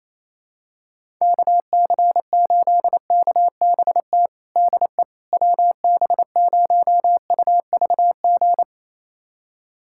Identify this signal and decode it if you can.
Morse code, heard as a keyed tone, about 28 wpm, 700 Hz. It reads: KC8KBT DE WB0UVG
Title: USA Callsign DE Callsign - 6 Characters - 28wpm